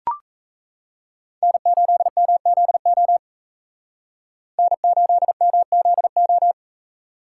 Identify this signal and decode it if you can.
Morse code, heard as a keyed tone, about 36 wpm, 700 Hz. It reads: N8MZO N8MZO